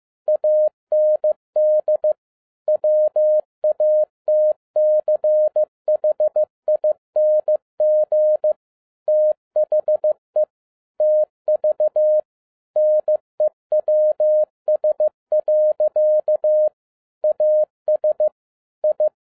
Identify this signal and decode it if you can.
Morse code, heard as a keyed tone, about 15 wpm, 610 Hz. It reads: AND WATCHING THE TV NEWS. AS I